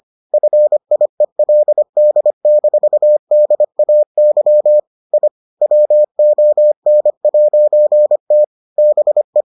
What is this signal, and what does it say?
Morse code, heard as a keyed tone, about 25 wpm, 600 Hz. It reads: FIELD-DAY I WON'T BE